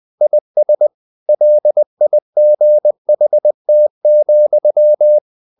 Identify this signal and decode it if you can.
Morse code, heard as a keyed tone, about 20 wpm, 600 Hz. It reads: IS LIGHT,
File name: the old man and the sea / the old man and the sea part 1 20wpm